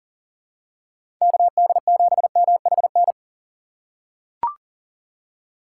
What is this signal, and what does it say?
Morse code, heard as a keyed tone, about 40 wpm, 700 Hz. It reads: KD7MHN
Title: USA Call Signs - 6 Characters - 4x Character Spacing - 40wpm